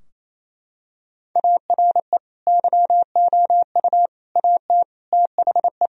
Morse code, heard as a keyed tone, about 28 wpm, 700 Hz. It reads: ARE YOU AT THE
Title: Sentences from Top 1000 Words - Review Part 2 - 28wpm